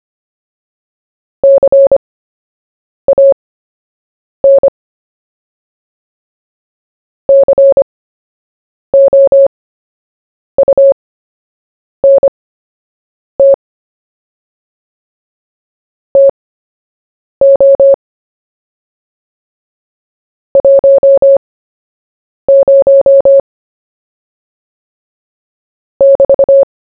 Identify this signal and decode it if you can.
Morse code, heard as a keyed tone, about 25 wpm, spaced as for 7 wpm, 550 Hz. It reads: CAN COUNT TO 10 <BT>
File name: cw / fists / Short-Story-1-7wpm